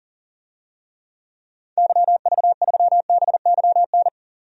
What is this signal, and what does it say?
Morse code, heard as a keyed tone, about 40 wpm, 700 Hz. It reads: YV3BYN
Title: DXCC-100 Call Signs - 6 Characters - 40wpm